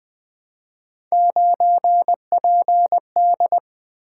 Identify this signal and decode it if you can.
Morse code, heard as a keyed tone, about 20 wpm, 700 Hz. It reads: 9PD
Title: Three Letter-Number Combinations - 20wpm